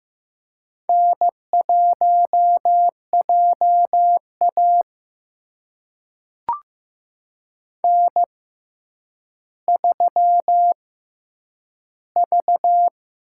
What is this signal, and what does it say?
Morse code, heard as a keyed tone, about 15 wpm, spaced as for 9 wpm, 700 Hz. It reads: N1JA N3V
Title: USA Call Signs - 4 Characters - 6x Character Spacing - 15wpm